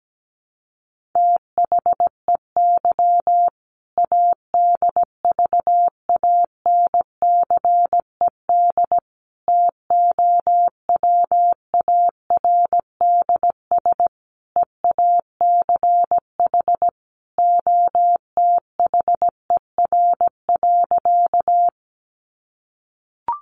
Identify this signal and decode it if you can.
Morse code, heard as a keyed tone, about 17 wpm, 700 Hz. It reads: THEY ADVANCED TOWARDS EACH OTHER.
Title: Sentences from Top 1200 Words 17wpm